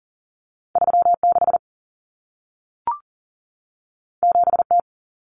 Morse code, heard as a keyed tone, about 40 wpm, 700 Hz. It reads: 36 7T
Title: Common Cut Numbers - 2-Digit 40wpm